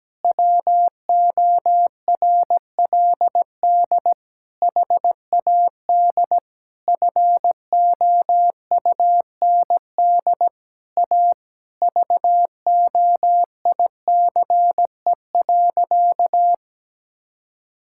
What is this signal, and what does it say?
Morse code, heard as a keyed tone, about 17 wpm, 700 Hz. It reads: WORLD HAD FOUND A VOICE.
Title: Sentences from Top 1600 Words 17wpm